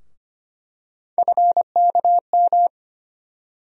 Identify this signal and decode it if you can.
Morse code, heard as a keyed tone, about 25 wpm, 700 Hz. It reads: FKM
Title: USA Call Sign Postfixes - 3 Letters 25wpm